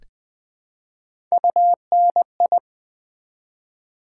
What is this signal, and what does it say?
Morse code, heard as a keyed tone, about 20 wpm, 700 Hz. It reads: UNI
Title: Build A Word - 1 Letter At A Time - 3 to 5-Letter Words - 20wpm